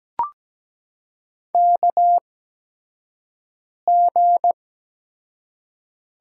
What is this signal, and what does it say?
Morse code, heard as a keyed tone, about 17 wpm, spaced as for 5 wpm, 700 Hz. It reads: KG
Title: USA Call Signs - 6 Characters - 8x Character Spacing - 17wpm